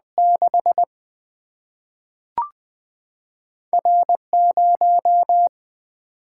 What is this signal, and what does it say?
Morse code, heard as a keyed tone, about 20 wpm, spaced as for 3 wpm, 700 Hz. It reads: 6R 0
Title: Single-Letter Single-Number 20wpm